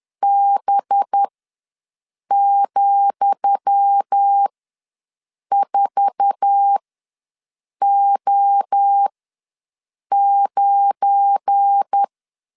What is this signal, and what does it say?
Morse code, heard as a keyed tone, about 10 wpm, spaced as for 6 wpm, 800 Hz. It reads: B,4O9
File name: LEZIONI / 10 SIGNS / 10D COMA ws 10wpm 4314